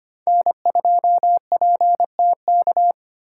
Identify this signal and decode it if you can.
Morse code, heard as a keyed tone, about 33 wpm, 700 Hz. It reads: N2PTK